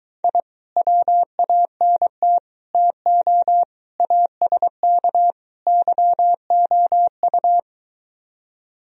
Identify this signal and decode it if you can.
Morse code, heard as a keyed tone, about 24 wpm, 700 Hz. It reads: I WANT TO ASK YOU